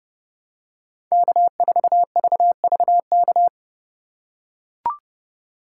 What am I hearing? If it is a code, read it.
Morse code, heard as a keyed tone, about 29 wpm, 700 Hz. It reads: K4VVK
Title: USA Call Signs - 5 Characters - 3x Character Spacing - 30wpm